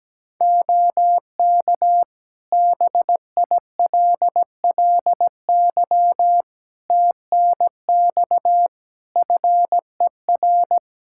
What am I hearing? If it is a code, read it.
Morse code, heard as a keyed tone, about 17 wpm, 700 Hz. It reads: OK BILLY TNX FER
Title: Practice QSO - Exchange 4 - Ending - 17wpm